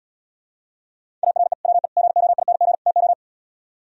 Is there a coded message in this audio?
Yes: Morse code from 1.2 to 3.1 s, heard as a keyed tone, about 77 wpm, 700 Hz. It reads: ARE WE GOING TO